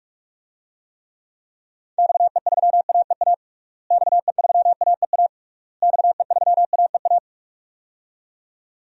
Morse code, heard as a keyed tone, about 44 wpm, 700 Hz. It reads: XE3AEA XE3AEA XE3AEA
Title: DXCC-100 Call Signs - 6 Characters - Repeated 3x - 45wpm